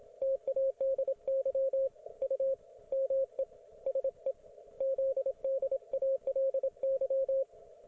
Morse code, heard as a keyed tone, about 27 wpm, 540 Hz, with radio noise. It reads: TADY U ME SE ZDALY